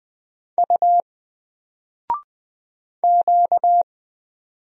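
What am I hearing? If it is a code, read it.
Morse code, heard as a keyed tone, about 20 wpm, 700 Hz. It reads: U Q
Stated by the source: Single letter and number Morse drill at 20 wpm, from N4LCD.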